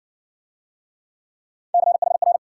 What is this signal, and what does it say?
Morse code, heard as a keyed tone, about 52 wpm, 700 Hz. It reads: CHR